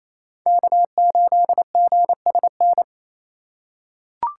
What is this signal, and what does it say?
Morse code, heard as a keyed tone, about 28 wpm, 700 Hz. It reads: K8GSN
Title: USA Call Signs - 5 Characters - 6x Character Spacing - 28wpm